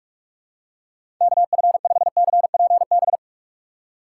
Morse code, heard as a keyed tone, about 46 wpm, 700 Hz. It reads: KF5CPB
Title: USA Call Signs - 6x Character Spacing - 45wpm